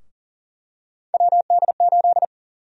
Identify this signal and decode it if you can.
Morse code, heard as a keyed tone, about 40 wpm, 700 Hz. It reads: WD8
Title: USA Call Sign Prefixes - Repeated 2x - 40wpm